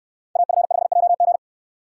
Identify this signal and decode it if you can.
Morse code, heard as a keyed tone, about 69 wpm, 700 Hz. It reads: AF5JG